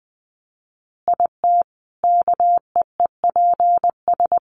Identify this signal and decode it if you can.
Morse code, heard as a keyed tone, about 20 wpm, 700 Hz. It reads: IT KEEPS
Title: Additional Easy Phrases 20wpm